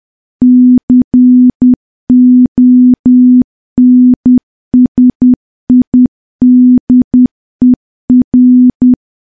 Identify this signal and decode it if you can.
Morse code, heard as a keyed tone, about 10 wpm, 250 Hz. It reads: CONSIDER
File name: morse / morse-training-10WPM